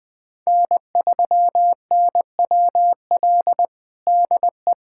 Morse code, heard as a keyed tone, about 20 wpm, 700 Hz. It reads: N3NWL DE
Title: USA Callsign DE Callsign - 5 Characters - 20wpm